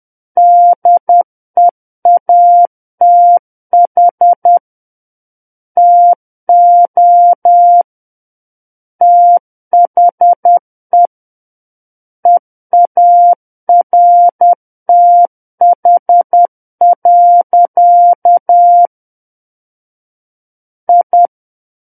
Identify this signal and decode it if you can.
Morse code, heard as a keyed tone, about 10 wpm, 700 Hz. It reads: DEATH TO THE EARTH. I
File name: Book One / 01-Chapter 1 - 10 WPM